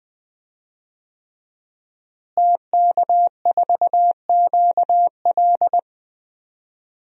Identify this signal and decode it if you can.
Morse code, heard as a keyed tone, about 21 wpm, 700 Hz. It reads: TK4QL
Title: DXCC-100 Call Signs - 5 Characters - Repeated 3x - 20wpm